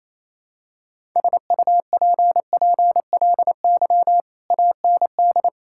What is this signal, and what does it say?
Morse code, heard as a keyed tone, about 27 wpm, 700 Hz. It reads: SUPPLY AND
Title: Sentences from Top 1000 Words - Review Part 6 - 28wpm